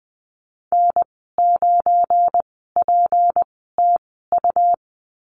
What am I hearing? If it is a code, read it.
Morse code, heard as a keyed tone, about 19 wpm, spaced as for 14 wpm, 700 Hz. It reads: N9PTU
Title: 186 Lesson 30g - Letter b - Call Signs 20wpm